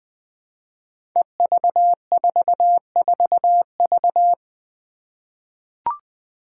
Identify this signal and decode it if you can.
Morse code, heard as a keyed tone, about 20 wpm, 700 Hz. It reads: EV44V
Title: Five Challenging Letter-Number Combinations 20wpm